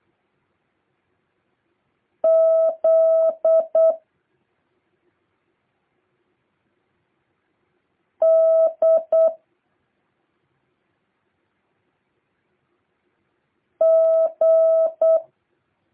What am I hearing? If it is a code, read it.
Morse code, heard as a keyed tone, about 8 wpm, 650 Hz. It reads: Z D G
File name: jh8cbh / 2023cw